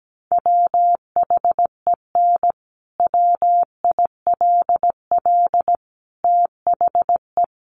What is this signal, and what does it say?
Morse code, heard as a keyed tone, about 17 wpm, 700 Hz. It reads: WHEN WILL THE
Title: Sentences from Top 1300 Words 17wpm